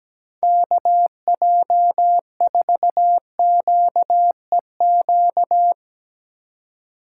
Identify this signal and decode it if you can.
Morse code, heard as a keyed tone, about 19 wpm, 700 Hz. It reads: KJ4QEQ
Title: Speed Racing - USA 2x3 Call Signs 17wpm